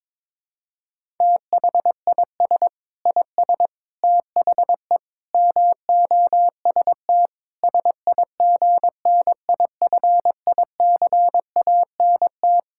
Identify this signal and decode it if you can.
Morse code, heard as a keyed tone, about 22 wpm, 700 Hz. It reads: THIS IS THE MOST SIGNIFICANT
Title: Sentences from Top 1000 Words - Review Part 6 - 22wpm